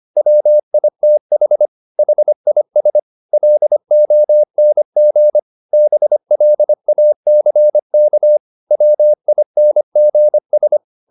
Morse code, heard as a keyed tone, about 25 wpm, 600 Hz. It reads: WITH HIS LONG BLACK WINGS
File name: the old man and the sea part 1 25wpm